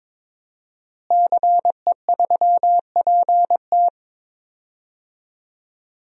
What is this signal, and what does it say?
Morse code, heard as a keyed tone, about 22 wpm, 700 Hz. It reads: CE3PT
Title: DXCC-100 Call Signs - 5 Characters - Repeated 2x - 22wpm